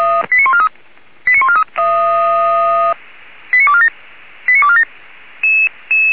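Five-tone selective call (ZVEI), heard as quick runs of five tones, tones of about 70 ms. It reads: 89143 89143 89247 89247
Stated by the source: Received by SDR and recorded by the Feuerwehr Orlamünde fire brigade.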